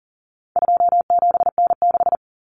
Morse code, heard as a keyed tone, about 40 wpm, 700 Hz. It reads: 27N6
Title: Common Cut Numbers - 4-Digit 40wpm